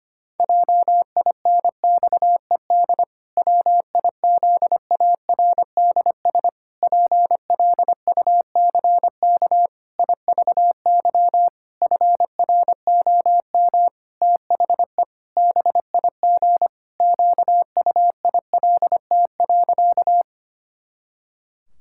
Morse code, heard as a keyed tone, about 25 wpm, 700 Hz. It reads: JINXED WIZARDS PLUCK IVY FROM THE BIG QUILT.
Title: Pangram Challenge v2 - 25wpm